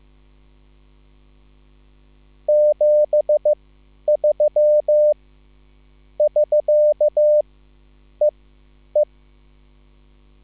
Morse code, heard as a keyed tone, about 15 wpm, spaced as for 10 wpm, 600 Hz. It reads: 73 <SK> E E
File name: QSL / CW lesson